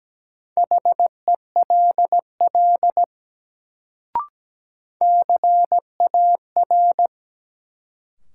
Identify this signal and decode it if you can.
Morse code, heard as a keyed tone, about 17 wpm, 700 Hz. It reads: HELL CAR
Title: Sets of 5 Words Encore - New Word Drill - 17wpm